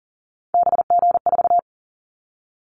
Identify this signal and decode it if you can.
Morse code, heard as a keyed tone, about 40 wpm, 700 Hz. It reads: BG4
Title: Callsign Prefixes for Top 100 DXCC - 40wpm